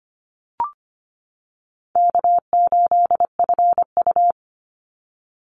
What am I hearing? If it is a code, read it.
Morse code, heard as a keyed tone, about 25 wpm, 700 Hz. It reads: K8FU
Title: Easy Call Signs - 25wpm